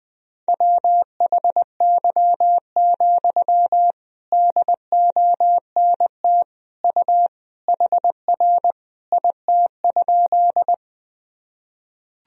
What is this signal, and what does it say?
Morse code, heard as a keyed tone, about 21 wpm, 700 Hz. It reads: WHY, DONT U HR IT?
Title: Sentences from Top 300 Words and Abbreviations 20wpm